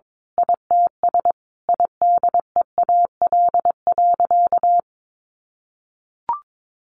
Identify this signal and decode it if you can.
Morse code, heard as a keyed tone, about 22 wpm, 700 Hz. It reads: ITS IDEAL.
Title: Sentences from Top 1400 Words 22wpm